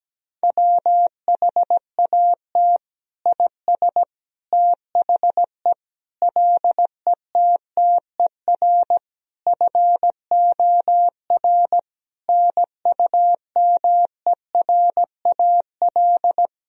Morse code, heard as a keyed tone, about 17 wpm, 700 Hz. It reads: WHAT IS THE LETTER FOR NUMERAL
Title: Sentences from Top 1000 Words - Review Part 2 - 17wpm